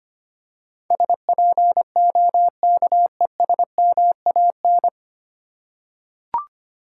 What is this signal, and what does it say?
Morse code, heard as a keyed tone, about 25 wpm, 700 Hz. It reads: SPOKESMAN
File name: practice / 25wpm / 9 Letter Words - First 3 Letters Given - 25wpm